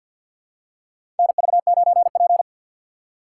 Morse code, heard as a keyed tone, about 52 wpm, 690 Hz. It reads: NV9P
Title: USA Call Signs - 2x Character Spacing - 50wpm